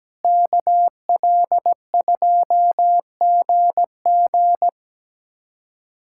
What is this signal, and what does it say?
Morse code, heard as a keyed tone, about 17 wpm, 700 Hz. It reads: KL2GG
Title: USA Callsign DE Callsign 17wpm